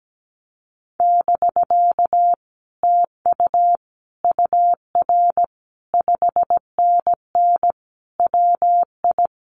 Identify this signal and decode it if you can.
Morse code, heard as a keyed tone, about 17 wpm, 700 Hz. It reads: <BK> TU UR 5NN WI
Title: Common QSO Phrases Encore 17wpm